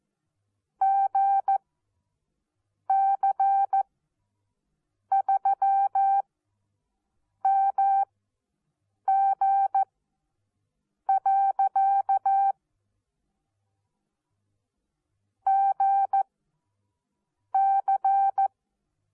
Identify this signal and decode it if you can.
Morse code, heard as a keyed tone, about 14 wpm, spaced as for 6 wpm, 780 Hz. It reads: GC3MG. GC
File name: Onlinelehrgang / morsen / DJ4UF Morsen14